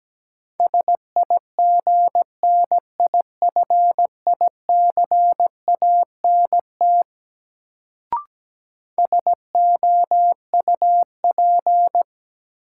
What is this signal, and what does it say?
Morse code, heard as a keyed tone, about 17 wpm, 700 Hz. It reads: SIGNIFICANT SOUP
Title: Sentences from Top 1000 Words - Review Part 6 - New Word Drill - 17wpm